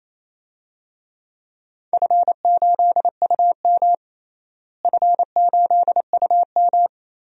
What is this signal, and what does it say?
Morse code, heard as a keyed tone, about 28 wpm, 700 Hz. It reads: F8UM F8UM
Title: DXCC-100 Call Signs - 4 Characters - Repeated 2x - 28wpm